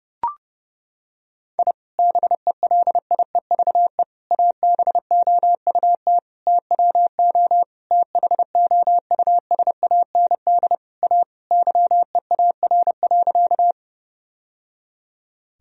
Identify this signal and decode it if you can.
Morse code, heard as a keyed tone, about 30 wpm, 700 Hz. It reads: I BELIEVE ABOUT TWO THOUSAND A YEAR.
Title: Sentences from Top 1000 Words - Review Part 3 - 30wpm